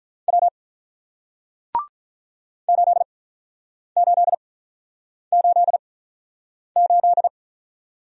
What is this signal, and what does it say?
Morse code, heard as a keyed tone, about 46 wpm, spaced as for 9 wpm, 700 Hz. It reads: W 8888